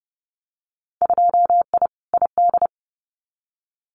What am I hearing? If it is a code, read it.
Morse code, heard as a keyed tone, about 30 wpm, 700 Hz. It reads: 2I ID
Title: Winter Field Day Exchanges Encore 30wpm